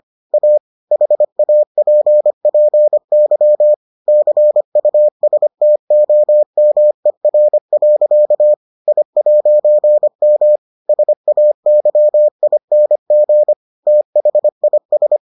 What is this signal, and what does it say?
Morse code, heard as a keyed tone, about 25 wpm, 600 Hz. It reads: A HAPPY CUSTOMER. I'M SAYING THIS